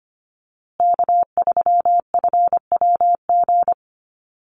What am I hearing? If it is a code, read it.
Morse code, heard as a keyed tone, about 25 wpm, 700 Hz. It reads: K3FWG
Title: USA Call Signs Repeated 3x 25wpm